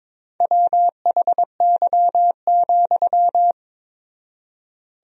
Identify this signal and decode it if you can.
Morse code, heard as a keyed tone, about 22 wpm, 700 Hz. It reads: WHY,
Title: 4x Word-Spacing - Sentences from Top 500 Words - 22wpm